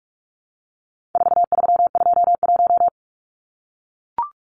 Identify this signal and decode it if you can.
Morse code, heard as a keyed tone, about 44 wpm, 700 Hz. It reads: 4321
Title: Numbers - 4-Digit 45wpm